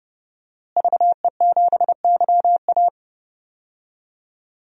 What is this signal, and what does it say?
Morse code, heard as a keyed tone, about 30 wpm, 700 Hz. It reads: VE7YA
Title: DXCC-100 Call Signs - 5 Characters - Repeated 2x - 30wpm